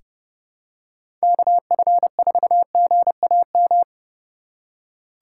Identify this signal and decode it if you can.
Morse code, heard as a keyed tone, about 32 wpm, 700 Hz. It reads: KF4GAM
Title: USA Call Signs - 6 Characters - 3x Character Spacing - 30wpm